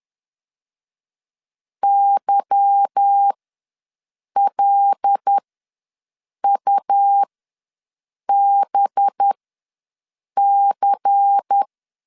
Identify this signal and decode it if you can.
Morse code, heard as a keyed tone, about 10 wpm, spaced as for 6 wpm, 800 Hz. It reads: YLUBC
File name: LEZIONI / 8 YJ45 / 8 YJ ws 10wpm 4043